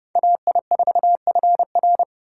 Morse code, heard as a keyed tone, about 36 wpm, 700 Hz. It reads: AI4FR
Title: Speed Racing - USA Call Signs 30wpm